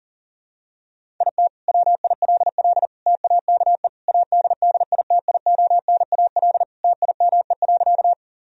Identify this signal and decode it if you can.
Morse code, heard as a keyed tone, about 40 wpm, 700 Hz. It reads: IT WILL TAKE ADDITIONAL TIME.
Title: Sentences from Top 1000 Words - Review Part 4 - 40wpm